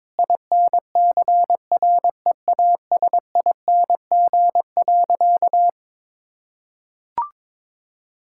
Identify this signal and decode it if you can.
Morse code, heard as a keyed tone, about 22 wpm, 700 Hz. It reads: INCREASING.
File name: practice / 22wpm / Sentences with 2-Word Phrases - 22wpm